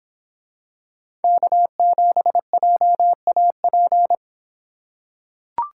Morse code, heard as a keyed tone, about 23 wpm, 700 Hz. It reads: K7JAP